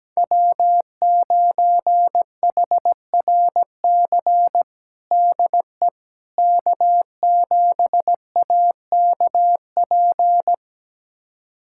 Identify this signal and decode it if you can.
Morse code, heard as a keyed tone, about 17 wpm, 700 Hz. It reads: W9HRC DE K7AKP
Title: USA Callsign DE Callsign - 5 Characters - 17wpm